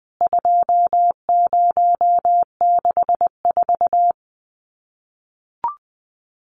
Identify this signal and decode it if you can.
Morse code, heard as a keyed tone, about 20 wpm, 700 Hz. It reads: 2064
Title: CWT Exchange 20wpm